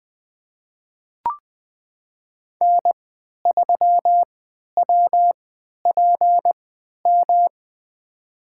Morse code, heard as a keyed tone, about 20 wpm, spaced as for 11 wpm, 700 Hz. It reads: N3WPM
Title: USA Call Signs - 5 Characters - 3x Character Spacing - 20wpm